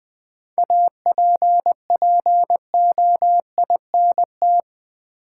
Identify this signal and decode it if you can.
Morse code, heard as a keyed tone, about 20 wpm, 700 Hz. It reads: APPOINT
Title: 7 Letter Words - First 3 Letters Given - 20wpm